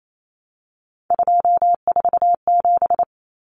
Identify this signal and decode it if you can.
Morse code, heard as a keyed tone, about 28 wpm, 700 Hz. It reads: 247
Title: Numbers - 3-Digit 28wpm